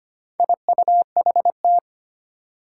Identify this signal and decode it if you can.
Morse code, heard as a keyed tone, about 25 wpm, 700 Hz. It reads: IUHT
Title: Four Challenging Letter-Number Combinations 25wpm